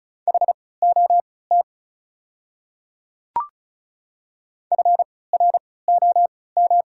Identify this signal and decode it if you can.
Morse code, heard as a keyed tone, about 36 wpm, spaced as for 20 wpm, 690 Hz. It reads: HOT FROM
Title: 3x Character Spacing - Top 100 Words - 35wpm